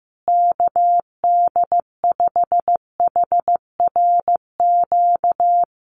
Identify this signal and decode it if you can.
Morse code, heard as a keyed tone, about 15 wpm, 700 Hz. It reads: KD5HRQ
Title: USA Call Signs Repeated 3x 15wpm